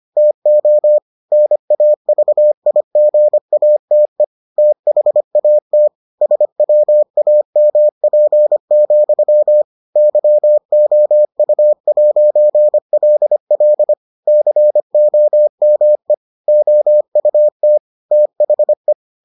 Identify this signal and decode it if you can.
Morse code, heard as a keyed tone, about 25 wpm, 600 Hz. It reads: TO NAVIGATE THAT SWAMP, YOU'LL COME OUT THE